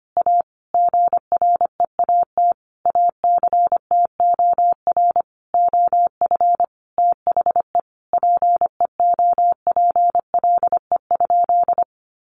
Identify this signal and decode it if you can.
Morse code, heard as a keyed tone, about 25 wpm, 700 Hz. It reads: A GREAT ACTOR OF THE PEOPLE?
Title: Sentences from Top 1500 Words 25wpm